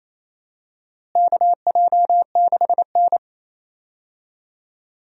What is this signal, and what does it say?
Morse code, heard as a keyed tone, about 28 wpm, 700 Hz. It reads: KJ6N
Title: USA Call Signs - 4 Characters - Repeated 3x - 28wpm